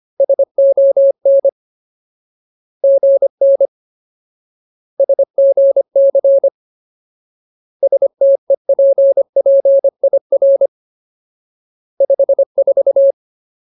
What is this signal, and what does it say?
Morse code, heard as a keyed tone, about 25 wpm, 550 Hz. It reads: SON GN SGC STEPPIR 54
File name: Ultimate Word List / 25WPM Ultimate